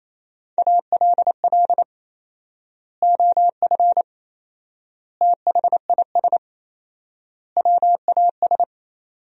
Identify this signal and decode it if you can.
Morse code, heard as a keyed tone, about 28 wpm, 700 Hz. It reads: ALL OF THIS WAS
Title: 4x Word-Spacing - Sentences from Top 500 Words - 28wpm